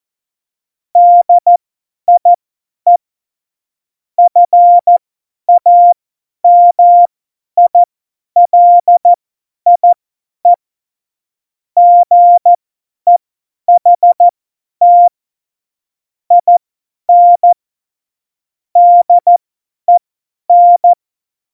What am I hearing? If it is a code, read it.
Morse code, heard as a keyed tone, about 14 wpm, spaced as for 10 wpm, 700 Hz. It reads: DIE FAMILIE GEHT IN DEN
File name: Morsekurs / Witze / Witz 19. September 17, Tempo 50